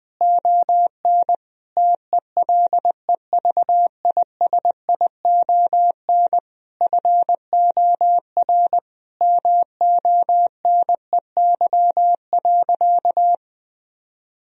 Morse code, heard as a keyed tone, about 20 wpm, 700 Hz. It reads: ON TELEVISION FOR MONEY.
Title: Sentences from Top 1000 Words - Review Part 4 - 20wpm